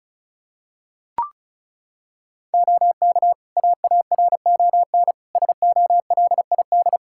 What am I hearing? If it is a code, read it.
Morse code, heard as a keyed tone, about 36 wpm, 700 Hz. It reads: OK AARON SOLID
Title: K1USN SST QSO Practice 35wpm